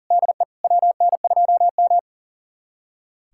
Morse code, heard as a keyed tone, about 40 wpm, 700 Hz. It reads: DE WN2M
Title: USA Callsign DE Callsign - 4 Characters - 40wpm